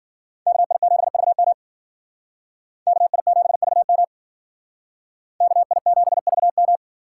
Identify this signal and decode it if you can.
Morse code, heard as a keyed tone, about 51 wpm, 700 Hz. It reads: KI7VM KI7VM KI7VM